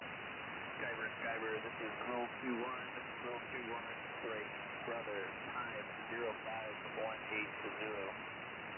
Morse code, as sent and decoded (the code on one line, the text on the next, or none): none